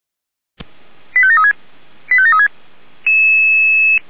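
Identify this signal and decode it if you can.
Five-tone selective call (ZVEI), heard as quick runs of five tones, tones of about 80 ms. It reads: 85736 85736
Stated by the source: ZVEI five-tone selective call 85909 received on an SDR.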